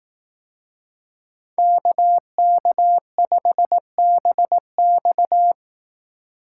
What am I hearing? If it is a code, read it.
Morse code, heard as a keyed tone, about 18 wpm, 700 Hz. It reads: KK5BX
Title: Speed Racing - USA Call Signs 15wpm